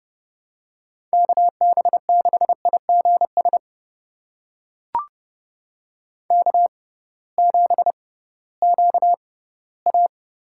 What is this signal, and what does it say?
Morse code, heard as a keyed tone, about 32 wpm, spaced as for 18 wpm, 700 Hz. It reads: KB6IGS K7QA
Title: USA Call Signs - 6x Character Spacing - 30wpm